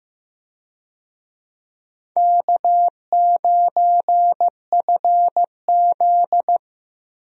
Morse code, heard as a keyed tone, about 15 wpm, 700 Hz. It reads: K9FZ
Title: USA Call Signs - 4 Characters - Repeated 2x - 15wpm